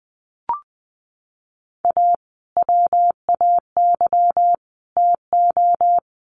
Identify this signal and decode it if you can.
Morse code, heard as a keyed tone, about 20 wpm, 700 Hz. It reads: A WAY TO
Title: Sets of 3 Words 20wpm